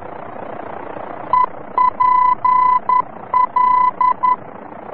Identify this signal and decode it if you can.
Morse code, heard as a keyed tone, about 11 wpm, 1000 Hz. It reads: EPL
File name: france / mx / EPL-VOR-F